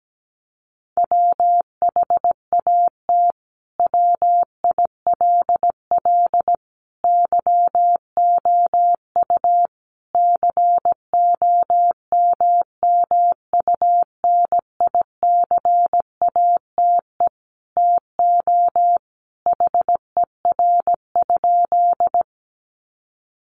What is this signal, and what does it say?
Morse code, heard as a keyed tone, about 17 wpm, 700 Hz. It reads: WHAT WILL YOU COMMUNICATE TO HER?
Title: Sentences from Top 1300 Words 17wpm